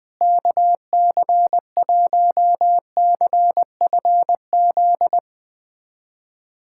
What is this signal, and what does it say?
Morse code, heard as a keyed tone, about 20 wpm, 700 Hz. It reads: KC1CFZ
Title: USA Callsign DE Callsign 20wpm